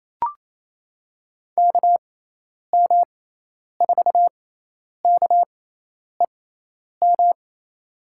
Morse code, heard as a keyed tone, about 27 wpm, spaced as for 10 wpm, 700 Hz. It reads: KM4KEM
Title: USA Call Signs - 6 Characters - 6x Character Spacing - 28wpm